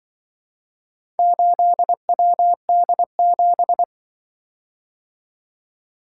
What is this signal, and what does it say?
Morse code, heard as a keyed tone, about 24 wpm, 700 Hz. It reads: 8WD7
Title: Speed-Racing - Four Letter-Number Combinations - 20wpm